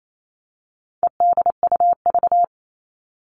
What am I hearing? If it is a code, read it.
Morse code, heard as a keyed tone, about 28 wpm, 700 Hz. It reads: EDUV
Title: All Cut Numbers - 4-Digit 28wpm